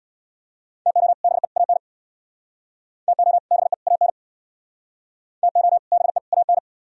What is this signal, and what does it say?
Morse code, heard as a keyed tone, about 62 wpm, 700 Hz. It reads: TO BE AN TO BE AN TO BE AN
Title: Speed-Racing - Sets of 3 Words 50wpm